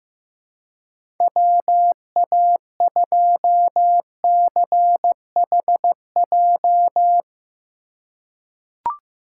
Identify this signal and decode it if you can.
Morse code, heard as a keyed tone, about 15 wpm, 700 Hz. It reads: WA2CHJ
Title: USA Call Signs - 4x Character Spacing - 15wpm